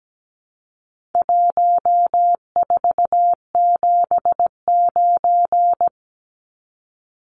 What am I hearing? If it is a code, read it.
Morse code, heard as a keyed tone, about 17 wpm, 700 Hz. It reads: 1479
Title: Numbers - Contest Style - 4-Digit 17wpm